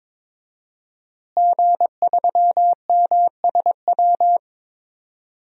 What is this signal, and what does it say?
Morse code, heard as a keyed tone, about 22 wpm, 700 Hz. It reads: G3MSW
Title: DXCC-100 Call Signs - 5 Characters - Repeated 3x - 22wpm